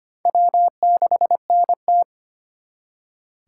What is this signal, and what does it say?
Morse code, heard as a keyed tone, about 33 wpm, 700 Hz. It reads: W6NT